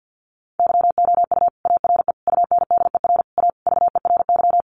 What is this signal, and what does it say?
Morse code, heard as a keyed tone, about 50 wpm, 700 Hz. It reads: YOU ARE UNDER A VERY
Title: Sentences from Top 1200 Words 50wpm